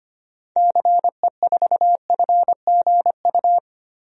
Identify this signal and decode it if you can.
Morse code, heard as a keyed tone, about 25 wpm, 700 Hz. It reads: CE4FGU
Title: DXCC-100 Call Signs - 6 Characters - 25wpm